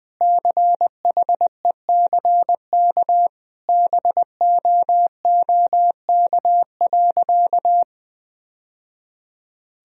Morse code, heard as a keyed tone, about 20 wpm, 700 Hz. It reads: CHECK BOOK.
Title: Sentences from Top 1000 Words - Review Part 3 - 20wpm